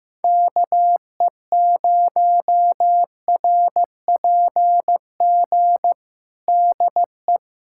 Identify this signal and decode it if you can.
Morse code, heard as a keyed tone, about 15 wpm, 700 Hz. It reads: KE0RPG DE
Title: USA Callsign DE Callsign - 6 Characters - 15wpm